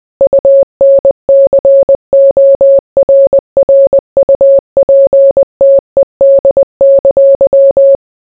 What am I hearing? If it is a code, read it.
Morse code, heard as a keyed tone, about 20 wpm, 550 Hz. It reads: UNCORRUPTED!